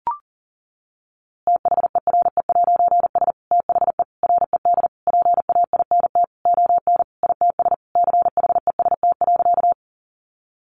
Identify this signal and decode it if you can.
Morse code, heard as a keyed tone, about 37 wpm, 700 Hz. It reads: THERE'S THE RED PAINT ON ITS CHEST.
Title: A Sound Of Thunder 40wpm